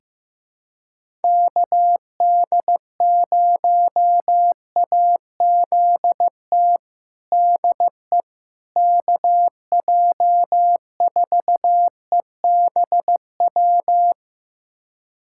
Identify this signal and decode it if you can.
Morse code, heard as a keyed tone, about 15 wpm, 700 Hz. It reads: KD0AZT DE KJ4EBW
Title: USA Callsign DE Callsign 15wpm